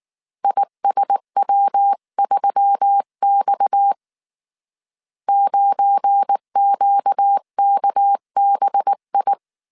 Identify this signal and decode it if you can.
Morse code, heard as a keyed tone, about 19 wpm, 800 Hz. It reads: ISW3X 9QKBI
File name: CW / DET / 1 9 1500C S B 18wpm 2551